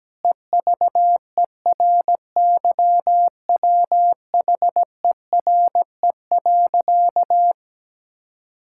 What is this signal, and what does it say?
Morse code, heard as a keyed tone, about 17 wpm, 700 Hz. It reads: EVERYWHERE.
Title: Sentences from Top 1600 Words 17wpm